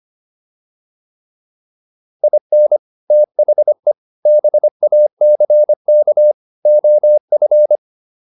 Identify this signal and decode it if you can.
Morse code, heard as a keyed tone, about 25 wpm, 600 Hz. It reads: IN THE BACK OF
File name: WD304 25